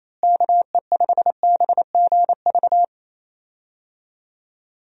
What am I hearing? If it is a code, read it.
Morse code, heard as a keyed tone, about 27 wpm, 700 Hz. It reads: KE5BGV
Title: USA Call Signs - 6 Characters - Repeated 2x - 28wpm